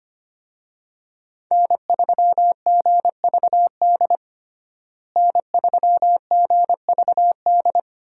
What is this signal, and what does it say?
Morse code, heard as a keyed tone, about 25 wpm, 700 Hz. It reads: N3GVD N3GVD
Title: USA Call Signs - 5 Characters - Repeated 2x - 25wpm